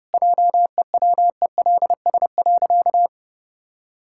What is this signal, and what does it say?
Morse code, heard as a keyed tone, about 32 wpm, 700 Hz. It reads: JEWELS.
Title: Pangram Challenge v2 - 30wpm